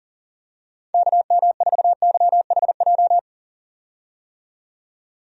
Morse code, heard as a keyed tone, about 37 wpm, 700 Hz. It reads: KM4YHJ
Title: USA Call Signs - 6 Characters - 40wpm